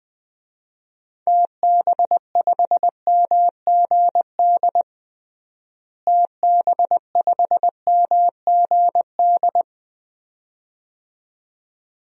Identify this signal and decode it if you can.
Morse code, heard as a keyed tone, about 21 wpm, 700 Hz. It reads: TB5MGD TB5MGD
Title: DXCC-100 Call Signs - 6 Characters - Repeated 2x - 20wpm